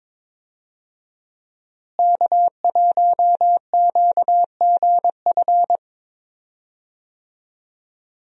Morse code, heard as a keyed tone, about 21 wpm, 700 Hz. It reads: K1QGF
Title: USA Call Signs - 5 Characters - Repeated 3x - 22wpm